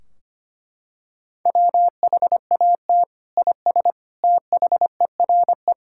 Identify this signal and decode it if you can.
Morse code, heard as a keyed tone, about 25 wpm, 700 Hz. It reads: WHAT IS THERE
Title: Sentences from Top 1600 Words 25wpm